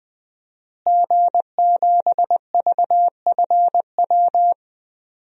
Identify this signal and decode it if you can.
Morse code, heard as a keyed tone, about 20 wpm, 700 Hz. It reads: G7VFW
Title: DXCC-100 Call Signs - 5 Characters - Repeated 3x - 20wpm